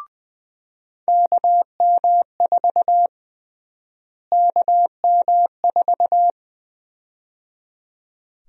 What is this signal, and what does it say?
Morse code, heard as a keyed tone, about 20 wpm, 700 Hz. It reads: KM4 KM4
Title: USA Call Sign Prefixes - Repeated 2x - 20wpm